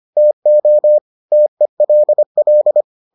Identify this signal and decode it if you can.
Morse code, heard as a keyed tone, about 25 wpm, 600 Hz. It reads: TO TELL